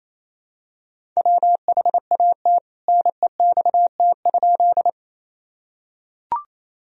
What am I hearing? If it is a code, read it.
Morse code, heard as a keyed tone, about 28 wpm, 700 Hz. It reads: WHAT NEXT?
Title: Sentences from Top 1000 Words - Review Part 1 - 28wpm